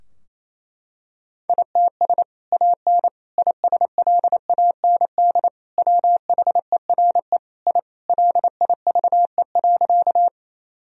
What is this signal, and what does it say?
Morse code, heard as a keyed tone, about 27 wpm, 700 Hz. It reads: ITS AN ISLAND WHERE I LIVE.
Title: Sentences from Top 1000 Words - Review Part 3 - 28wpm